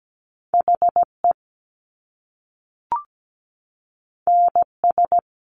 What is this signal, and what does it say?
Morse code, heard as a keyed tone, about 17 wpm, 700 Hz. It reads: HE NS
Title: Top Two Letter Combinations in Words 17wpm